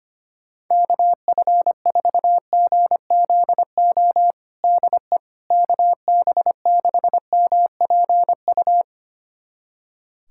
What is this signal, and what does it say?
Morse code, heard as a keyed tone, about 25 wpm, 700 Hz. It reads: KF4GZO DE KB6MPU
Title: USA Callsign DE Callsign - 6 Characters - 25wpm